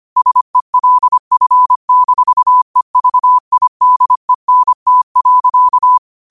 Morse code, heard as a keyed tone, about 25 wpm, 1000 Hz. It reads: SELF-EVIDENT.